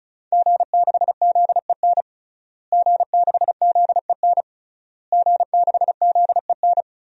Morse code, heard as a keyed tone, about 36 wpm, 700 Hz. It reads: G6ZEN G6ZEN G6ZEN
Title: DXCC-100 Call Signs - 5 Characters - Repeated 3x - 35wpm